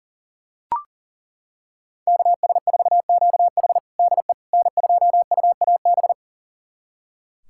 Morse code, heard as a keyed tone, about 37 wpm, 700 Hz. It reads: KS4QH DE N2UAB
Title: USA Callsign DE Callsign - 5 Characters - 40wpm